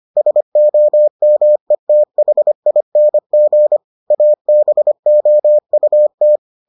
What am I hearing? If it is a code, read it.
Morse code, heard as a keyed tone, about 25 wpm, 600 Hz. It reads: SOMETHING ABOUT